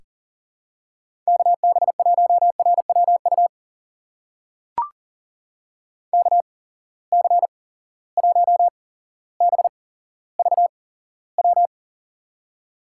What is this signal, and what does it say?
Morse code, heard as a keyed tone, about 37 wpm, spaced as for 17 wpm, 700 Hz. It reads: KB1RWU KC1BVW
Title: USA Call Signs - 6 Characters - 8x Character Spacing - 40wpm